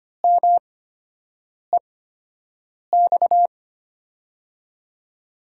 Morse code, heard as a keyed tone, about 25 wpm, spaced as for 7 wpm, 700 Hz. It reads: MEX
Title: USA Call Signs - 5 Characters - 8x Character Spacing - 25wpm